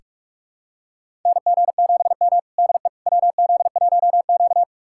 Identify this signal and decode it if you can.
Morse code, heard as a keyed tone, about 46 wpm, 700 Hz. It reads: NG7M DE WZ1Q
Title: USA Callsign DE Callsign - 4 Characters - 45wpm